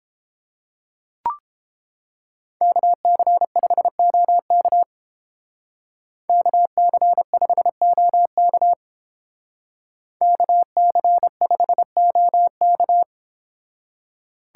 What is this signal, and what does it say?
Morse code, heard as a keyed tone, about 31 wpm, 700 Hz. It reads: KC5OK KC5OK KC5OK